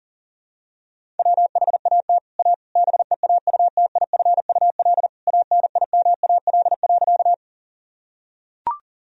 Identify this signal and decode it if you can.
Morse code, heard as a keyed tone, about 40 wpm, 700 Hz. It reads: WHAT A BEAUTIFUL ANIMAL.
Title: Sentences from Top 1000 Words - Review Part 1 - 40wpm